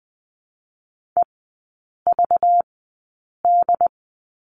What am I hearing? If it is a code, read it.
Morse code, heard as a keyed tone, about 20 wpm, 700 Hz. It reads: E V D
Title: 218 Lesson 35e - Number 0 - Three Characters 20wpm